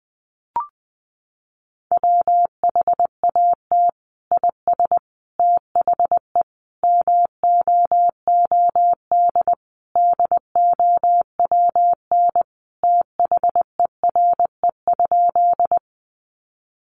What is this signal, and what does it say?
Morse code, heard as a keyed tone, about 20 wpm, 700 Hz. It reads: WHAT IS THE MOOD DOWN THERE?
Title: Sentences from Top 1100 Words 20wpm